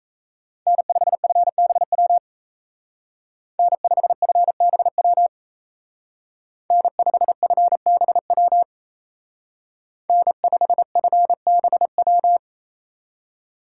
Morse code, heard as a keyed tone, about 33 wpm, 700 Hz. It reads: N5FBW N5FBW N5FBW N5FBW
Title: Speed Racing - USA 1x3 Call Signs 28wpm